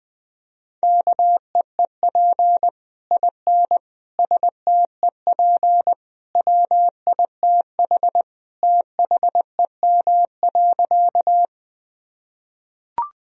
Morse code, heard as a keyed tone, about 20 wpm, 700 Hz. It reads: KEEP IN STEP WITH THEM.
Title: Sentences from Top 1000 Words - Review Part 2 - 20wpm